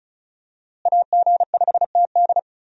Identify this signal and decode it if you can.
Morse code, heard as a keyed tone, about 36 wpm, 700 Hz. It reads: AG5TD
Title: USA Call Signs - 5 Characters - Repeated 2x - 35wpm